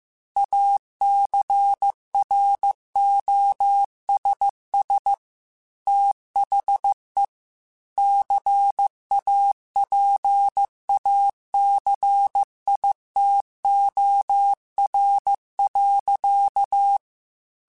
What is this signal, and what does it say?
Morse code, heard as a keyed tone, about 15 wpm, 800 Hz. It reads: ACROSS THE CAPACITOR.
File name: Technical Subjects / Technical Subjects-mp3s / Reactance @13